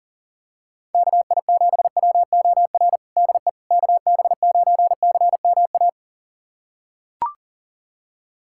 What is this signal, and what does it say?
Morse code, heard as a keyed tone, about 40 wpm, 700 Hz. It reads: KI7WOR DE KB9CMA
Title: USA Callsign DE Callsign - 6 Characters - 40wpm